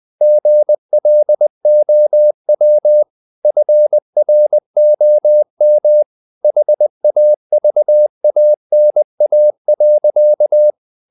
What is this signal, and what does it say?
Morse code, heard as a keyed tone, about 20 wpm, 600 Hz. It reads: GLOW FROM HAVANA.
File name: the old man and the sea / the old man and the sea part 1 20wpm